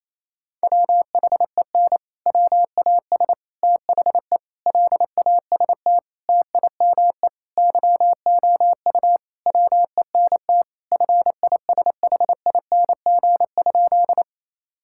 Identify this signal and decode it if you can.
Morse code, heard as a keyed tone, about 28 wpm, 700 Hz. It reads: WHEN WAS THE LAST TIME YOU WENT FISHING?
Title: Sentences from Top 1000 Words - Review Part 5 - 28wpm